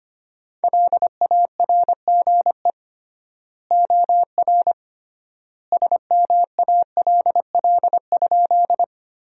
Morse code, heard as a keyed tone, about 25 wpm, 700 Hz. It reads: LARGE OR SMALL?
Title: 3x Word-Spacing - Sentences from Top 200 Words - 25wpm